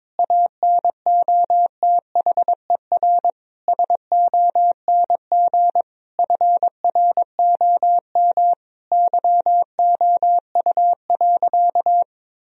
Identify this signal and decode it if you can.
Morse code, heard as a keyed tone, about 22 wpm, 700 Hz. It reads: ANOTHER SONG FROM YOU.
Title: Sentences from Top 1000 Words - Review Part 2 - 22wpm